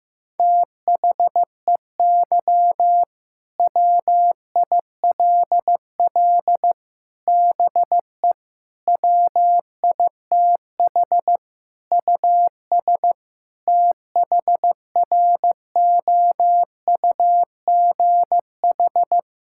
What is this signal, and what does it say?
Morse code, heard as a keyed tone, about 16 wpm, 700 Hz. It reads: THEY WILL BE WITH US THROUGH
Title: Sentences from Top 1600 Words 15wpm